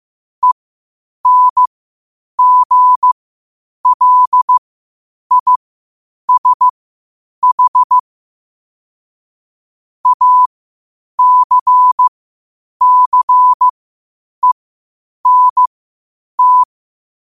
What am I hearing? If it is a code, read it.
Morse code, heard as a keyed tone, about 15 wpm, spaced as for 8 wpm, 1000 Hz. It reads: ENGLISH ACCENT
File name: qotdfiles / 8wpm